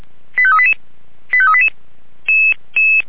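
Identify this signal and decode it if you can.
Five-tone selective call (ZVEI), heard as quick runs of five tones, tones of about 70 ms. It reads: 85388 85388
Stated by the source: ZVEI five-tone selective call received on an SDR.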